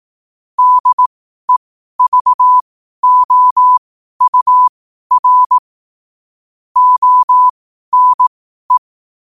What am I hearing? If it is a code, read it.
Morse code, heard as a keyed tone, about 18 wpm, spaced as for 12 wpm, 1000 Hz. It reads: DEVOUR ONE
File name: qotdfiles / 12wpm